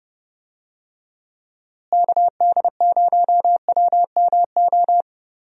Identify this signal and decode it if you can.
Morse code, heard as a keyed tone, about 30 wpm, 700 Hz. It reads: KD0WMO
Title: USA Call Signs - 6 Characters - Repeated 3x - 30wpm